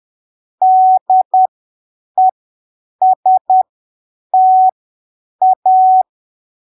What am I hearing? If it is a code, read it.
Morse code, heard as a keyed tone, about 10 wpm, spaced as for 7 wpm, 750 Hz. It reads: DESTA
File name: arquivos / texto / texto 27 10ppm